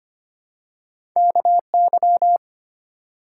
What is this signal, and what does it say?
Morse code, heard as a keyed tone, about 25 wpm, 700 Hz. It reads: KY
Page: morse-code-ninja-practice-25wpm